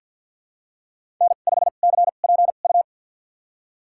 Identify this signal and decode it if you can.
Morse code, heard as a keyed tone, about 50 wpm, spaced as for 36 wpm, 700 Hz. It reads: N5CPU
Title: USA Call Signs - 2x Character Spacing - 50wpm